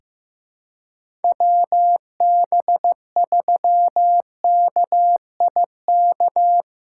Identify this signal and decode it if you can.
Morse code, heard as a keyed tone, about 15 wpm, 700 Hz. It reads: WB3KIK
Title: USA Call Signs - 6 Characters - 6x Character Spacing - 15wpm